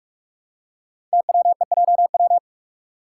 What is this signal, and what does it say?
Morse code, heard as a keyed tone, about 44 wpm, 690 Hz. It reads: TWEJW
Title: Code Groups - 5 Letters - 45wpm